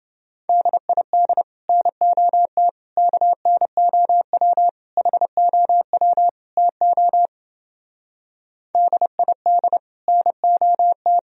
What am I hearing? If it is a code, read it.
Morse code, heard as a keyed tone, about 29 wpm, 700 Hz. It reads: DID NOT KNOW HOW TO DID NOT